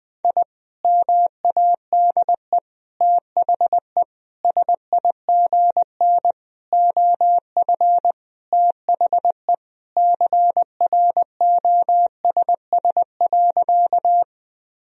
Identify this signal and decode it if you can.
Morse code, heard as a keyed tone, about 21 wpm, 700 Hz. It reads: I MADE THE SIGN OF THE CROSS.
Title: Sentences from Top 1000 Words - Review Part 1 - 20wpm